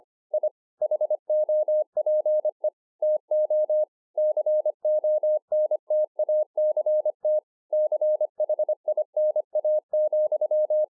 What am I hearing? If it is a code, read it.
Morse code, heard as a keyed tone, about 26 wpm, 610 Hz. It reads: I HOPE TO CONTACT CHINA,
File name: 2011-04-mp3 / BEGINNERS DIARY 25WPM